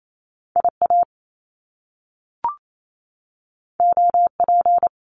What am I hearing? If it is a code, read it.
Morse code, heard as a keyed tone, about 28 wpm, 700 Hz. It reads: IA OP
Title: Top Two Letter Combinations in Words 28wpm